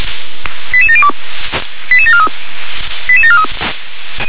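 Five-tone selective call (ZVEI), heard as quick runs of five tones, tones of about 70 ms. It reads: 88702 88743 88743
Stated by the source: ZVEI five-tone selective call received on an SDR.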